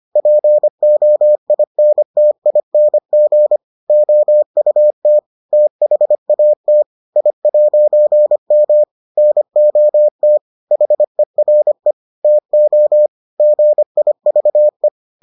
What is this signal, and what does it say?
Morse code, heard as a keyed tone, about 25 wpm, 600 Hz. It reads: POINTING OUT THAT I'M NOT HERE TO GIVE